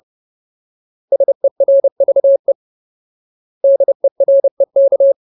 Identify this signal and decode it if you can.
Morse code, heard as a keyed tone, about 30 wpm, 550 Hz. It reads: SERVE DEREK
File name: Ultimate Word List / 30WPM Ultimate